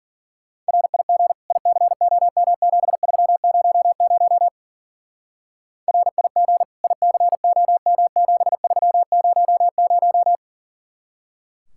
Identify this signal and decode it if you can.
Morse code, heard as a keyed tone, about 49 wpm, 700 Hz. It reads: RIG ICOM7300 RIG ICOM7300
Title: Speed-Racing Common QSO Phrases Encore - 40wpm